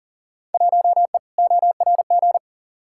Morse code, heard as a keyed tone, about 40 wpm, 700 Hz. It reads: 1E ORG
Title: Field Day Exchanges - 40wpm